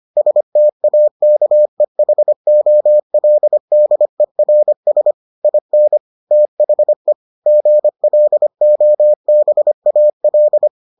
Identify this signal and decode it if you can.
Morse code, heard as a keyed tone, about 25 wpm, 600 Hz. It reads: STAKEHOLDERS IN THE GLOBAL